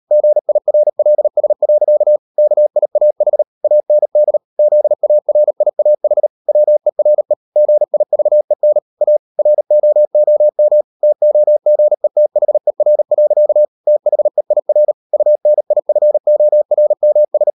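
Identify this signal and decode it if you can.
Morse code, heard as a keyed tone, about 38 wpm, 600 Hz. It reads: GIRLS. KIAH AND ZARIAH WERE GIVEN A ROOM TOGETHER. THEIR UNIFORMS
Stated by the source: Morse code practice at 38 WPM, sending Ken Follett's novel Never in English, from CW med Gnister.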